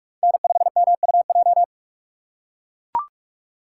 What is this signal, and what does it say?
Morse code, heard as a keyed tone, about 44 wpm, 700 Hz. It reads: N5MUJ